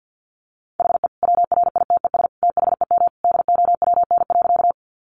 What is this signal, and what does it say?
Morse code, heard as a keyed tone, about 50 wpm, 700 Hz. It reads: HE WRITES THEM DOWN.
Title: Additional Easy Phrases 50wpm